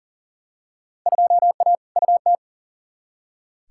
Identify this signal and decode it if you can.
Morse code, heard as a keyed tone, about 40 wpm, 700 Hz. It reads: 2A UT
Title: Field Day Exchanges - 40wpm